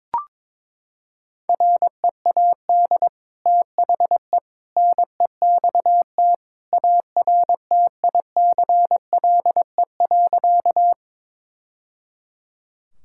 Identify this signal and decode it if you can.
Morse code, heard as a keyed tone, about 22 wpm, 700 Hz. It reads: READ THE NEXT ARTICLE.
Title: Sentences from Top 1000 Words - Review Part 5 - 22wpm